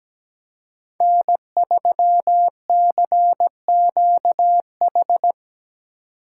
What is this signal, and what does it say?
Morse code, heard as a keyed tone, about 17 wpm, 700 Hz. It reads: N3CQH
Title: USA Call Signs - 5 Characters - 8x Character Spacing - 17wpm